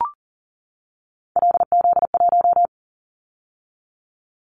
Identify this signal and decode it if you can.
Morse code, heard as a keyed tone, about 38 wpm, 700 Hz. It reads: LZ1
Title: Callsign Prefixes for Top 100 DXCC - 40wpm